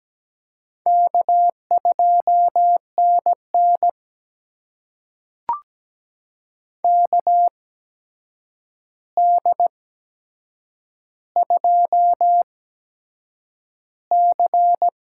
Morse code, heard as a keyed tone, about 17 wpm, spaced as for 7 wpm, 700 Hz. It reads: K2NN KD2C
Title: USA Call Signs - 4 Characters - 8x Character Spacing - 17wpm